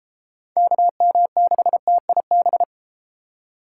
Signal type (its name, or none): Morse code